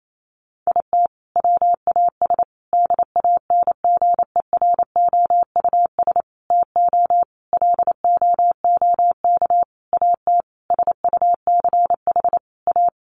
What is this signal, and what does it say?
Morse code, heard as a keyed tone, about 28 wpm, 700 Hz. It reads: IT WAS DANGEROUS TO LOOK AT SUCH A
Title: Sentences from Top 1300 Words 28wpm